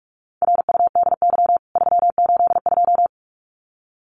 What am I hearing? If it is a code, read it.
Morse code, heard as a keyed tone, about 44 wpm, 700 Hz. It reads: RUDY 382
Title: CWT Exchange 45wpm